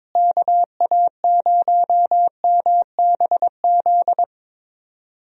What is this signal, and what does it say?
Morse code, heard as a keyed tone, about 22 wpm, 700 Hz. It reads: KA0MBZ
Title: USA Call Signs - 6 Characters - Repeated 2x - 22wpm